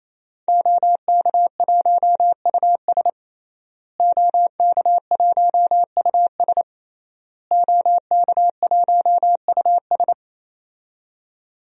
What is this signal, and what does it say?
Morse code, heard as a keyed tone, about 28 wpm, 700 Hz. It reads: OK1US OK1US OK1US
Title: DXCC-100 Call Signs - 5 Characters - Repeated 3x - 28wpm